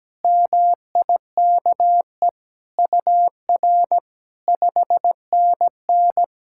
Morse code, heard as a keyed tone, about 17 wpm, 700 Hz. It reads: MIKE UR 5NN
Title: SOTA QSO Practice 17wpm